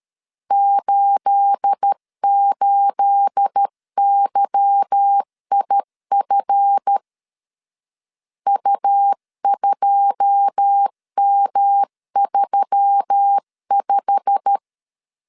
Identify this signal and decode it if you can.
Morse code, heard as a keyed tone, about 13 wpm, 800 Hz. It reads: 88YIF U2M35
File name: LEZIONI / 9 28 12wpm 2351